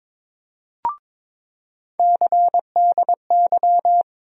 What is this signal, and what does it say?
Morse code, heard as a keyed tone, about 22 wpm, 700 Hz. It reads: CDY
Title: USA Call Sign Postfixes - 3 Letters 22wpm